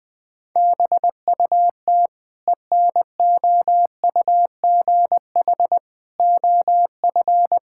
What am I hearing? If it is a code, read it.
Morse code, heard as a keyed tone, about 20 wpm, 700 Hz. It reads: BUT ENOUGH OF
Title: Sentences from Top 1000 Words - Review Part 5 - 20wpm